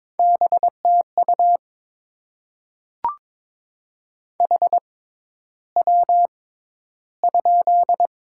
Morse code, heard as a keyed tone, about 22 wpm, spaced as for 11 wpm, 700 Hz. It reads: BTU HW?
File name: Top 20 QSO Elements - 6x Character Spacing - 22wpm